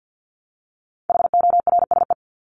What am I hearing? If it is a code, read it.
Morse code, heard as a keyed tone, about 50 wpm, 700 Hz. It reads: HORSE
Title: Top 300 Words Repeated 50wpm